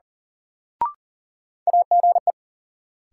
Morse code, heard as a keyed tone, about 37 wpm, 700 Hz. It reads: AGE
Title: Sets of 3 Words - New Word Drill - 40wpm